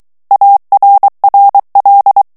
Morse code, heard as a keyed tone, about 23 wpm, 800 Hz. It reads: ARRL
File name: ANGLAIS texte clair 3 vitesse 22 mots minute